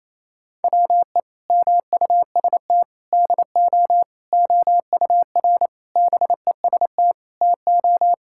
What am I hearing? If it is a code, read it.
Morse code, heard as a keyed tone, about 27 wpm, 700 Hz. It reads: WE MUST DO OUR BEST TO
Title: Sentences from Top 1000 Words - Review Part 5 - 28wpm